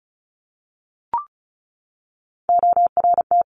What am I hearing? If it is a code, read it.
Morse code, heard as a keyed tone, about 36 wpm, 700 Hz. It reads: ORT
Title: Top Three Letter Combinations in Words 35wpm